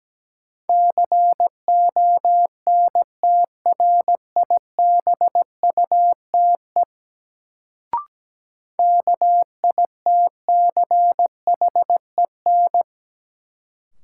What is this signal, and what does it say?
Morse code, heard as a keyed tone, about 17 wpm, 700 Hz. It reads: CONTRIBUTE KITCHEN
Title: Sentences from Top 1400 Words - New Word Drill Alt - 17wpm